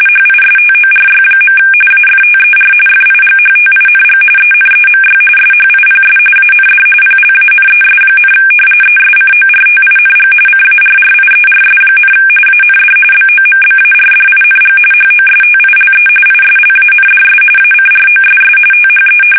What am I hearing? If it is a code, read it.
RTTY, about 45.45 baud. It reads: HISTORY. AFTER FIGHTING ENEMY SHIPS AND PLANES, ESCORTING CARRIERS, AND BOMBARDING HOSTILE ISLANDS ACROSS THE
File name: fsk850 / battleshipnj